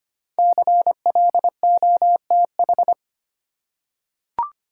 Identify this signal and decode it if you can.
Morse code, heard as a keyed tone, about 25 wpm, 700 Hz. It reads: CLOTH
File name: 5 Letter Words - First 3 Letters Given - 25wpm